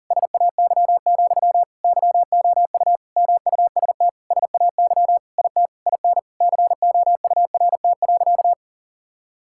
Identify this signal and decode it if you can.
Morse code, heard as a keyed tone, about 40 wpm, 700 Hz. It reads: SAY, YOU MUST SAY IT IN COURT.
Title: Sentences from Top 1600 Words 40wpm